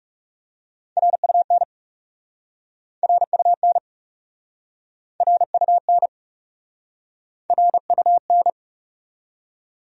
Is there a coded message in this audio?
Yes: Morse code from 1.0 to 8.5 s, heard as a keyed tone, about 42 wpm, 700 Hz. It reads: RUN RUN RUN RUN